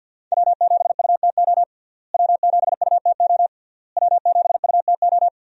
Morse code, heard as a keyed tone, about 50 wpm, 700 Hz. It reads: W7UTO W7UTO W7UTO
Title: USA Call Signs - 5 Characters - Repeated 3x - 50wpm